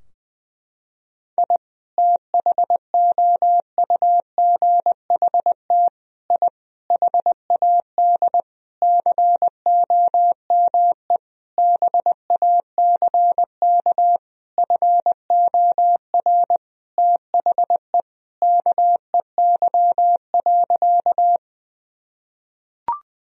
Morse code, heard as a keyed tone, about 21 wpm, 700 Hz. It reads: I THOUGHT I HAD COME BACK FOR THE KEY.
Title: Sentences from Top 1000 Words - Review Part 4 - 20wpm